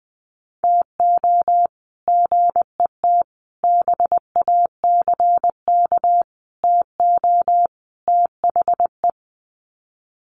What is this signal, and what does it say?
Morse code, heard as a keyed tone, about 20 wpm, 700 Hz. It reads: TO GET BACK TO THE
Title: Sets of 5 Words Encore 20wpm